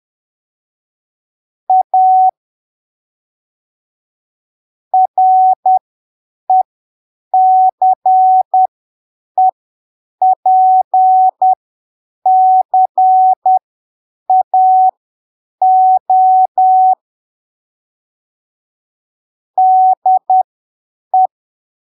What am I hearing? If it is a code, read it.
Morse code, heard as a keyed tone, about 10 wpm, spaced as for 7 wpm, 750 Hz. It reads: A RECEPCAO DE
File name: arquivos / texto / texto 18 10ppm